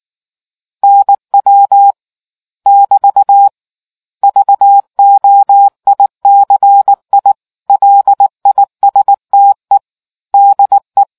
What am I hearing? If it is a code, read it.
Morse code, heard as a keyed tone, about 19 wpm, 800 Hz. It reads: NW <BT> VOICI LISTE DE